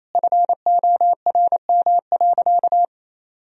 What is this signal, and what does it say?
Morse code, heard as a keyed tone, about 28 wpm, 700 Hz. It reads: FORM.
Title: Sentences from Top 1000 Words - Review Part 1 - 28wpm